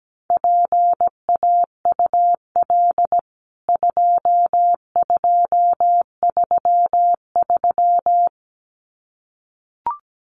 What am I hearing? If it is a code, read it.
Morse code, heard as a keyed tone, about 17 wpm, 700 Hz. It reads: PAUL 2233
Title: CWT Exchange 17wpm